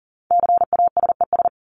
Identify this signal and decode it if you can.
Morse code, heard as a keyed tone, about 40 wpm, 700 Hz. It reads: CASES
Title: ES:S Suffix - Top 100 Words with Base Words 40wpm